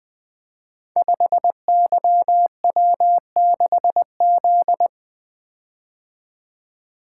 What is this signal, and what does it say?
Morse code, heard as a keyed tone, about 20 wpm, 700 Hz. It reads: 5YW6Z
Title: Code Groups - 5 Letter:Numbers - 20wpm